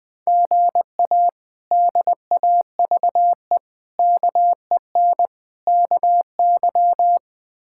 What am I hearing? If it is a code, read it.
Morse code, heard as a keyed tone, about 20 wpm, 700 Hz. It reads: GA DAVE KEN KY
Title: K1USN SST QSO Practice - 20wpm